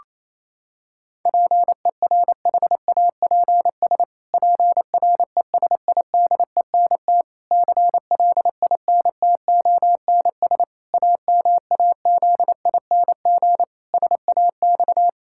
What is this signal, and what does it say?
Morse code, heard as a keyed tone, about 28 wpm, 700 Hz. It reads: PERHAPS PRESIDENT CLINTONS AMAZING SAX
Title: Pangram Challenge v2 - 28wpm